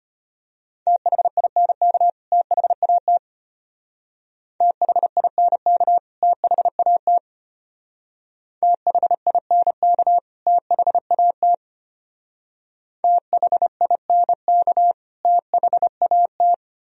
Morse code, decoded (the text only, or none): THINK THAT THINK THAT THINK THAT THINK THAT